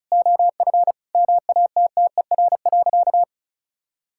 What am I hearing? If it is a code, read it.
Morse code, heard as a keyed tone, about 36 wpm, 700 Hz. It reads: OF MATTER.
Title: Sentences from Top 1000 Words - Review Part 5 - 35wpm